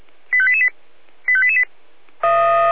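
Five-tone selective call (ZVEI), heard as quick runs of five tones, tones of about 70 ms. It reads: 85888 85888
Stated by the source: ZVEI five-tone selective call received on an SDR.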